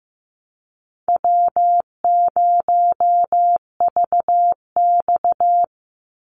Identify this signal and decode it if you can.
Morse code, heard as a keyed tone, about 15 wpm, 700 Hz. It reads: W0VX
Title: Call Signs Repeated 3x - 15wpm